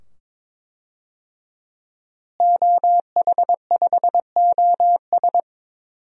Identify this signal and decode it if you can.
Morse code, heard as a keyed tone, about 22 wpm, 700 Hz. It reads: OH5OS
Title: DXCC-100 Call Signs - 5 Characters - Repeated 2x - 22wpm